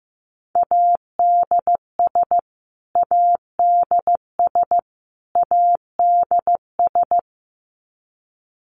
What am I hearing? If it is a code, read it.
Morse code, heard as a keyed tone, about 15 wpm, 700 Hz. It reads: ADS ADS ADS
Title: Abbreviations Repeated 3x - 15wpm